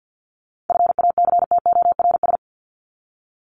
Morse code, heard as a keyed tone, about 50 wpm, 700 Hz. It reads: FACTORS
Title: ES:S Suffix - Top 100 Words with Base Words 50wpm